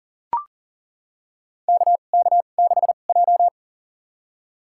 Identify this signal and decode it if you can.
Morse code, heard as a keyed tone, about 40 wpm, spaced as for 29 wpm, 700 Hz. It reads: KK6J
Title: USA Call Signs - 4 Characters - 2x Character Spacing - 40wpm